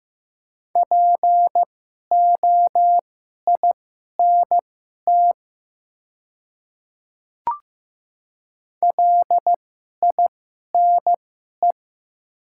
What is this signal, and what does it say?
Morse code, heard as a keyed tone, about 15 wpm, spaced as for 11 wpm, 700 Hz. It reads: POINT LINE
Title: 2x Character Spacing - Top 200 Words - 15wpm